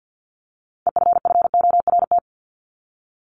Morse code, heard as a keyed tone, about 50 wpm, 700 Hz. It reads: EFFORT
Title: Top 1001-1250 Words 50wpm